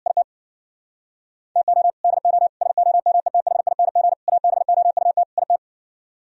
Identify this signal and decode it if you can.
Morse code, heard as a keyed tone, about 64 wpm, 700 Hz. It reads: IT TO DO SOMETHING ABOUT IT